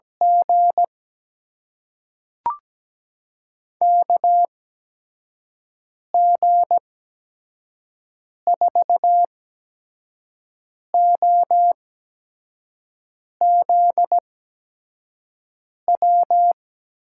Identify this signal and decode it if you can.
Morse code, heard as a keyed tone, about 17 wpm, spaced as for 5 wpm, 700 Hz. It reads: G KG4OZW
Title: USA Call Signs - 6 Characters - 8x Character Spacing - 17wpm